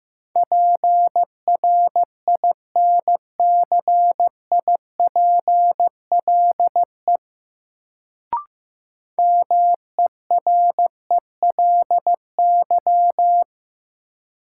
Morse code, heard as a keyed tone, about 15 wpm, 700 Hz. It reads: PRINCIPLE MERELY
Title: Sentences from Top 1400 Words - New Word Drill Alt - 15wpm